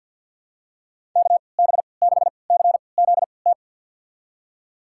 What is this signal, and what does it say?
Morse code, heard as a keyed tone, about 48 wpm, spaced as for 28 wpm, 700 Hz. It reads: KB6CZT
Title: USA Call Signs - 6 Characters - 3x Character Spacing - 50wpm